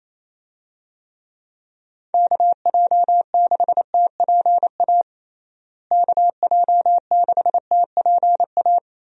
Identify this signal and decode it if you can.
Morse code, heard as a keyed tone, about 28 wpm, 700 Hz. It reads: KJ6TPA KJ6TPA
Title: USA Call Signs - 6 Characters - Repeated 2x - 28wpm